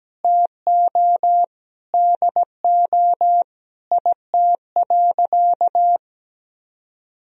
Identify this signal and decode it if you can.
Morse code, heard as a keyed tone, about 17 wpm, 700 Hz. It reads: TO DO IT.
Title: Sentences from Top 1000 Words - Review Part 3 - 17wpm